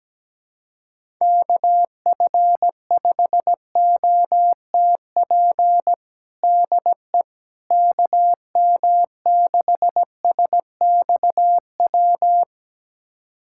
Morse code, heard as a keyed tone, about 17 wpm, 700 Hz. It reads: KF5OTP DE KM6SXW
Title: USA Callsign DE Callsign 17wpm